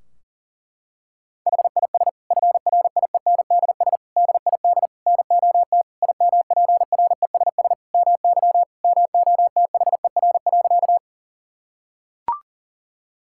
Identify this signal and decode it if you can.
Morse code, heard as a keyed tone, about 40 wpm, 700 Hz. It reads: HIS FRIENDS DID NOT IMPRESS MY MOTHER.
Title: Sentences from Top 1600 Words 40wpm